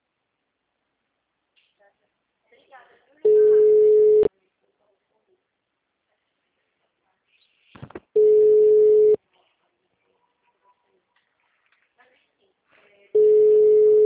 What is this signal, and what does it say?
Ringback tone (Europe, 425 Hz), with speech.
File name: out- -2058- - - .5078